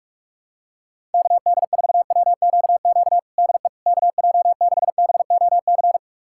Morse code, heard as a keyed tone, about 46 wpm, 700 Hz. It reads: KD4WQQ DE KJ6BOC
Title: USA Callsign DE Callsign - 6 Characters - 45wpm